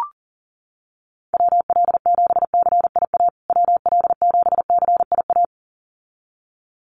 Morse code, heard as a keyed tone, about 40 wpm, 700 Hz. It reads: WL7CIA WL7CIA
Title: USA Call Signs Repeated 2x 40wpm